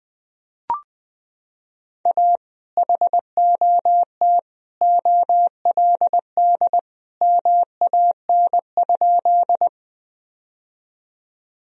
Morse code, heard as a keyed tone, about 21 wpm, 700 Hz. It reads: A HOT OLD MAN?
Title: Sentences from Top 1000 Words - Review Part 3 - 20wpm